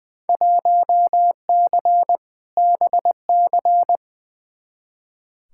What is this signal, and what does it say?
Morse code, heard as a keyed tone, about 20 wpm, 700 Hz. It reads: 1C BC
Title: Field Day Exchanges Encore - 20wpm